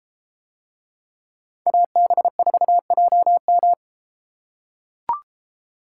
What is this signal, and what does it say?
Morse code, heard as a keyed tone, about 31 wpm, 700 Hz. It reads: AB4JM